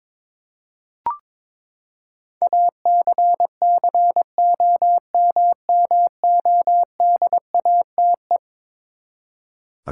Morse code, heard as a keyed tone, about 22 wpm, 700 Hz. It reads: ACCOMMODATE
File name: practice / 22wpm / Eleven Letter Words - 22wpm